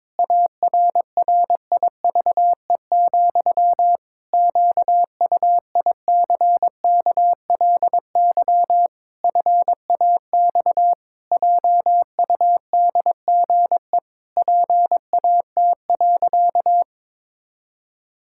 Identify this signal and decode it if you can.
Morse code, heard as a keyed tone, about 22 wpm, 700 Hz. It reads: ARRIVE, QUICKLY FAX JUDGE PAT.
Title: Pangram Challenge v2 - 22wpm